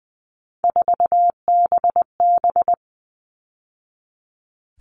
Morse code, heard as a keyed tone, about 20 wpm, 700 Hz. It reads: 4BB
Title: All Cut Numbers - 3-Digit 20wpm